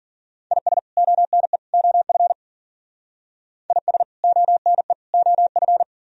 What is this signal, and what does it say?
Morse code, heard as a keyed tone, about 51 wpm, 700 Hz. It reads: IS ONE OF IS ONE OF